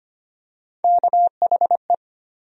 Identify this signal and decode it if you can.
Morse code, heard as a keyed tone, about 25 wpm, 700 Hz. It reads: KHE
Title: USA Call Sign Postfixes - 3 Letters 25wpm